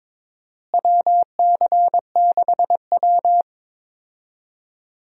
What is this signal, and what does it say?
Morse code, heard as a keyed tone, about 21 wpm, 700 Hz. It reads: WC6W
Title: USA Call Signs - 4 Characters - Repeated 2x - 22wpm